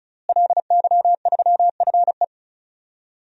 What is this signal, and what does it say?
Morse code, heard as a keyed tone, about 36 wpm, 700 Hz. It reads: LY3FE
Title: DXCC-100 Call Signs - 5 Characters - Repeated 2x - 35wpm